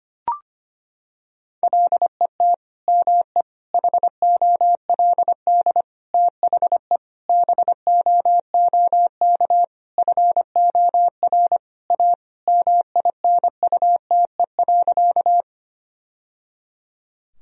Morse code, heard as a keyed tone, about 25 wpm, 700 Hz. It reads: LET ME HOLD THE BOOK FOR A MINUTE.
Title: Sentences from Top 1000 Words - Review Part 2 - 25wpm